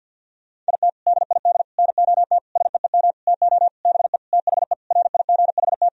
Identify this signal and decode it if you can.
Morse code, heard as a keyed tone, about 48 wpm, 700 Hz. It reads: IT DID NOT SEEM TO BE THE RIGHT
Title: Sentences from Top 1000 Words - Review Part 2 - 50wpm